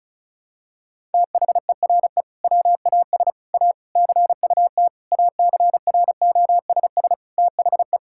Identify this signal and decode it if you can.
Morse code, heard as a keyed tone, about 36 wpm, 700 Hz. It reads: THERE WAS A CUT ACROSS THE
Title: Sentences from Top 1000 Words - Review Part 5 - 35wpm